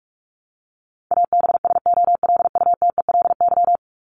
Morse code, heard as a keyed tone, about 44 wpm, 700 Hz. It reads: ABSOLUTELY
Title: Top 1501-1750 Words 45wpm